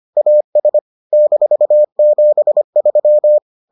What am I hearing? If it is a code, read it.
Morse code, heard as a keyed tone, about 25 wpm, 600 Hz. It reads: AS -73